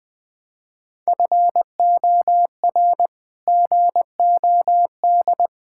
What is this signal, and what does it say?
Morse code, heard as a keyed tone, about 21 wpm, 700 Hz. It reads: FOR GOD
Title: Sets of 8 Words - 20wpm